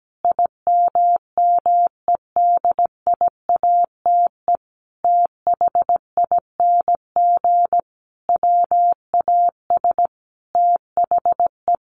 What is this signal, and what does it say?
Morse code, heard as a keyed tone, about 17 wpm, 700 Hz. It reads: IMMEDIATE THING WAS THE
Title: A Sound Of Thunder 17wpm